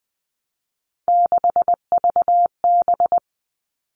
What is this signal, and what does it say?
Morse code, heard as a keyed tone, about 20 wpm, 700 Hz. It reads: 6VB
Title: All Cut Numbers - 3-Digit 20wpm